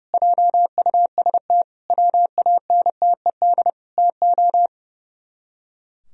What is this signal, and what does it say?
Morse code, heard as a keyed tone, about 38 wpm, 700 Hz. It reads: JUST WANTED TO